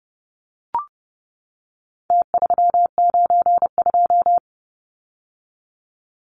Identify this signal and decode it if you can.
Morse code, heard as a keyed tone, about 30 wpm, 700 Hz. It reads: T392
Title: Numbers - Contest Style - 4-Digit 30wpm